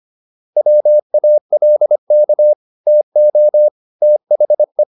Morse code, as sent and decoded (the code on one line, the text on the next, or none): .-- .- .-.. -.- / - --- / - .... .
WALK TO THE